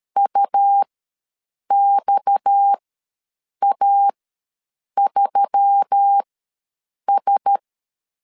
Morse code, heard as a keyed tone, about 13 wpm, spaced as for 7 wpm, 800 Hz. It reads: UXA3S
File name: LEZIONI / 6 PKX0 / 6 PKX ws 12wpm 3106